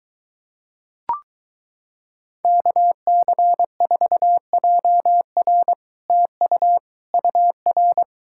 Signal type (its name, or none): Morse code